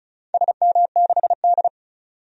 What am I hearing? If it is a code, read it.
Morse code, heard as a keyed tone, about 34 wpm, 700 Hz. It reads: SM6D
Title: DXCC-100 Call Signs - 4 Characters - 35wpm